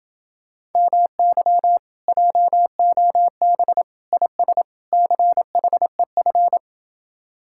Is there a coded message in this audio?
Yes: Morse code from 0.8 to 6.6 s, heard as a keyed tone, about 31 wpm, 700 Hz. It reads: MY JOB IS CHEF